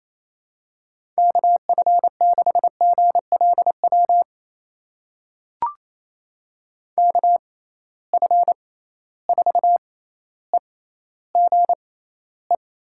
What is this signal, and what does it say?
Morse code, heard as a keyed tone, about 27 wpm, spaced as for 14 wpm, 700 Hz. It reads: KF6GLW KF4EGE
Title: USA Call Signs - 6 Characters - 6x Character Spacing - 28wpm